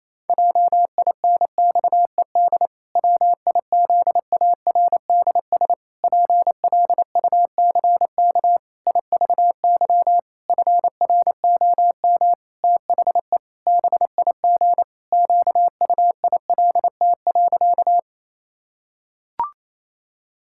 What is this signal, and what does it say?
Morse code, heard as a keyed tone, about 28 wpm, 700 Hz. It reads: JINXED WIZARDS PLUCK IVY FROM THE BIG QUILT.
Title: Pangram Challenge v2 - 28wpm